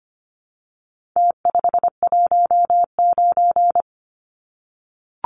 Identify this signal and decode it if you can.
Morse code, heard as a keyed tone, about 25 wpm, 700 Hz. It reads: T519
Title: Numbers - Contest Style - 4-Digit 25wpm